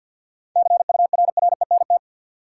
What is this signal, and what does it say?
Morse code, heard as a keyed tone, about 53 wpm, 700 Hz. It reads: CURRENT
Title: Sentences from Top 1600 Words - New Word Drill Alt - 50wpm